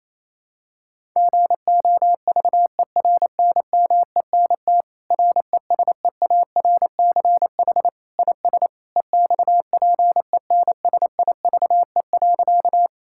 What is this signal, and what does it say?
Morse code, heard as a keyed tone, about 27 wpm, 700 Hz. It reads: GOVERNMENT RESEARCH IS EXPENSIVE.
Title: Sentences from Top 1000 Words - Review Part 6 - 28wpm